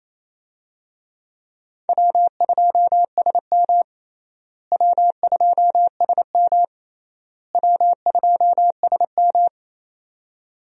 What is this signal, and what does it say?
Morse code, heard as a keyed tone, about 28 wpm, 700 Hz. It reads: W2SM W2SM W2SM
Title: USA Call Signs - 4 Characters - Repeated 3x - 28wpm